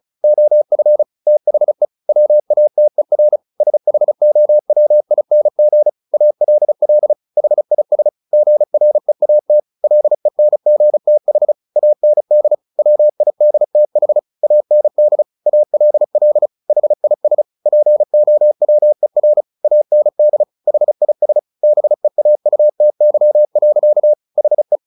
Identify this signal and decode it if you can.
Morse code, heard as a keyed tone, about 35 wpm, 600 Hz. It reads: OF THE WATER SHOWING ALL HIS GREAT LENGTH AND WIDTH AND ALL HIS POWER AND HIS BEAUTY. HE
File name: the old man and the sea 2 35wpm